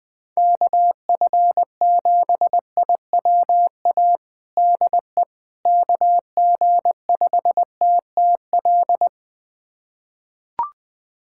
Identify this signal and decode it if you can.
Morse code, heard as a keyed tone, about 20 wpm, 700 Hz. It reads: KF7IWA DE KG5TTL
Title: USA Callsign DE Callsign 20wpm